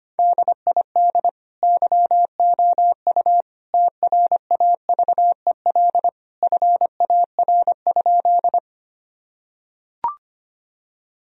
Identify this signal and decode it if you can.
Morse code, heard as a keyed tone, about 25 wpm, 700 Hz. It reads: DID YOU TRAVEL FAR?
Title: Sentences from Top 1000 Words - Review Part 2 - 25wpm